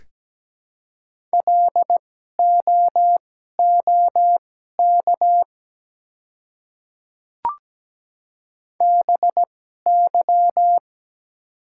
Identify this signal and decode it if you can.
Morse code, heard as a keyed tone, about 17 wpm, spaced as for 12 wpm, 700 Hz. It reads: LOOK BY
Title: 2x Character Spacing - Top 100 Words - 17wpm